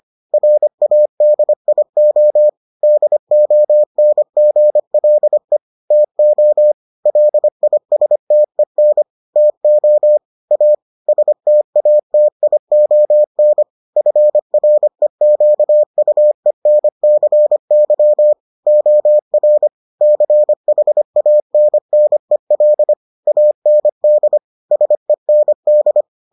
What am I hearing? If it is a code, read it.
Morse code, heard as a keyed tone, about 25 wpm, 600 Hz. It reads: RADIO DONGLE TO LISTEN TO A STATION FREQUENCY OR CHANNEL AND SEND